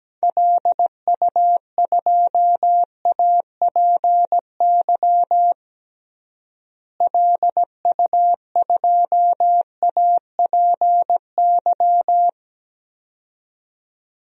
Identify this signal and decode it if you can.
Morse code, heard as a keyed tone, about 17 wpm, 700 Hz. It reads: LU2APY LU2APY
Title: DXCC-100 Call Signs - 6 Characters - Repeated 2x - 17wpm